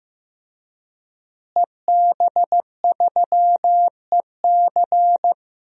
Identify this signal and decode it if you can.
Morse code, heard as a keyed tone, about 15 wpm, 700 Hz. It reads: EB3EC
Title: DXCC-100 Call Signs - 5 Characters - Repeated 2x - 15wpm